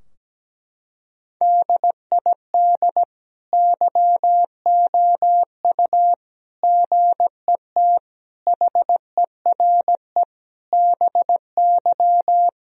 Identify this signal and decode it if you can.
Morse code, heard as a keyed tone, about 17 wpm, 700 Hz. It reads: DID YOU GET HERE BY
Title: Sentences from Top 1000 Words - Review Part 3 - 17wpm